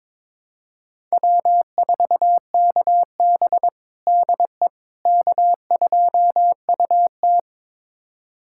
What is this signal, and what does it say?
Morse code, heard as a keyed tone, about 22 wpm, 700 Hz. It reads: W4KB DE K2UT
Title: USA Callsign DE Callsign - 4 Characters - 22wpm